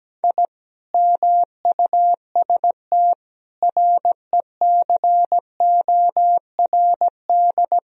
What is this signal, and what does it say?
Morse code, heard as a keyed tone, about 17 wpm, 700 Hz. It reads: I MUST RECORD
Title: Sentences from Top 1000 Words - Review Part 3 - 17wpm